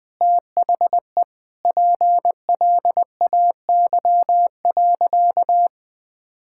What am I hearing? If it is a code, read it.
Morse code, heard as a keyed tone, about 20 wpm, 700 Hz. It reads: THE PLAY.
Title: Sentences with 4-Word Phrases - 20wpm